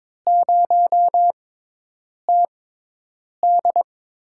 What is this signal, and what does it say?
Morse code, heard as a keyed tone, about 22 wpm, spaced as for 8 wpm, 700 Hz. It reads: 0TD
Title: USA Call Signs - 6x Character Spacing - 22wpm